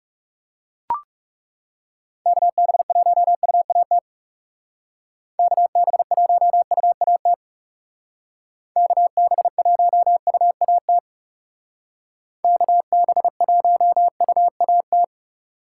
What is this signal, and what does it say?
Morse code, heard as a keyed tone, about 35 wpm, 700 Hz. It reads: KB1UAT KB1UAT KB1UAT KB1UAT
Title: Speed Racing - USA 2x3 Call Signs 30wpm